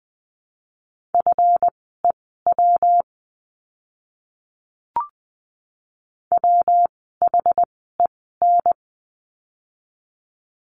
Morse code, heard as a keyed tone, about 19 wpm, spaced as for 14 wpm, 700 Hz. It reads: FEW WHEN
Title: 244 Lesson 39f - Letter z - Words 20wpm